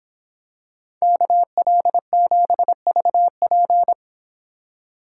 Morse code, heard as a keyed tone, about 26 wpm, 700 Hz. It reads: KL7VP